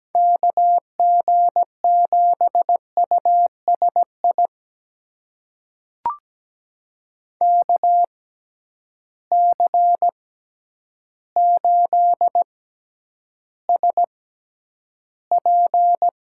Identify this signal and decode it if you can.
Morse code, heard as a keyed tone, about 17 wpm, spaced as for 9 wpm, 700 Hz. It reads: KG7USI KC8SP
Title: USA Call Signs - 6 Characters - 6x Character Spacing - 17wpm